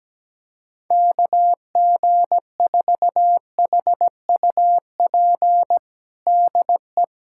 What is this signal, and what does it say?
Morse code, heard as a keyed tone, about 17 wpm, 700 Hz. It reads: KG4HUP DE
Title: USA Callsign DE Callsign - 6 Characters - 17wpm